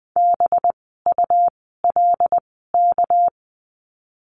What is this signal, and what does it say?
Morse code, heard as a keyed tone, about 20 wpm, spaced as for 14 wpm, 700 Hz. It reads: BULK
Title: 199 Lesson 32f - Letter k - Words 20wpm